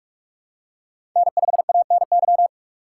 Morse code, heard as a keyed tone, about 42 wpm, 700 Hz. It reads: N5ANY
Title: USA Call Signs - 5 Characters - 4x Character Spacing - 45wpm